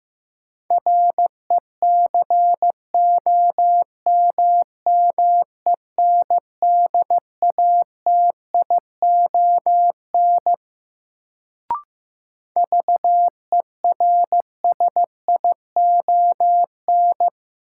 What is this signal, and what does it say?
Morse code, heard as a keyed tone, about 15 wpm, 700 Hz. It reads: RECOMMENDATION VERSION
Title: Sentences from Top 1400 Words - New Word Drill - 15wpm